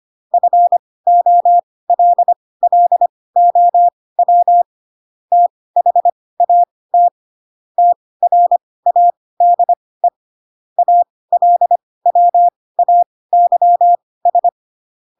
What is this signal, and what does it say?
Morse code, heard as a keyed tone, about 25 wpm, spaced as for 18 wpm, 700 Hz. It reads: FOLLOW THAT TRADE ALWAYS
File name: Pinocchio / Pinocchio ch4 12 25 ik1hge